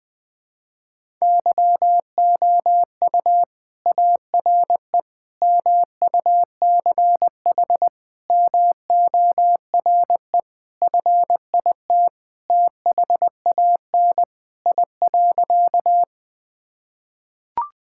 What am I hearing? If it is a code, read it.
Morse code, heard as a keyed tone, about 20 wpm, 700 Hz. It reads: YOU ARE MUCH MORE FIT THAN I.
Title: Sentences from Top 1000 Words - Review Part 5 - 20wpm